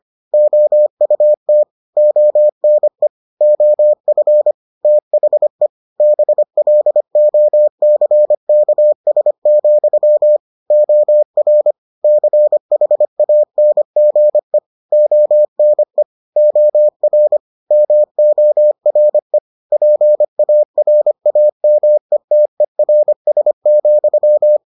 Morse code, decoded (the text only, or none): OUT ONE OF THE BLOCKS, OR CHANGE ONE OR MORE PARAMETERS,